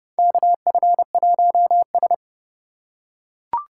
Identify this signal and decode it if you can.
Morse code, heard as a keyed tone, about 30 wpm, 700 Hz. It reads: KF1S
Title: USA Call Signs - 2x Character Spacing - 30wpm